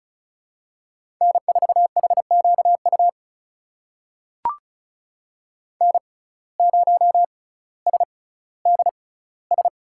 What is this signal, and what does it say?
Morse code, heard as a keyed tone, about 34 wpm, spaced as for 18 wpm, 700 Hz. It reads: N4HQU N0SDS
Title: USA Call Signs - 5 Characters - 6x Character Spacing - 35wpm